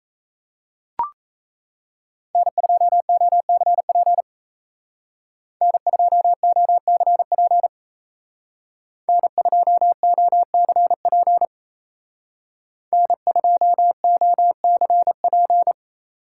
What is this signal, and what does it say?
Morse code, heard as a keyed tone, about 36 wpm, 700 Hz. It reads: N2OCP N2OCP N2OCP N2OCP